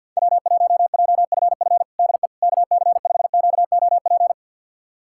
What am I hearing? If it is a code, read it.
Morse code, heard as a keyed tone, about 48 wpm, 700 Hz. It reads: W1JFF DE KC5QOP
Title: USA Callsign DE Callsign 50wpm